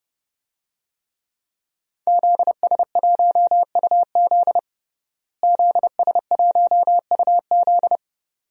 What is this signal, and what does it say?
Morse code, heard as a keyed tone, about 32 wpm, 700 Hz. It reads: ZS1UZ ZS1UZ
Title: DXCC-100 Call Signs - 5 Characters - Repeated 2x - 30wpm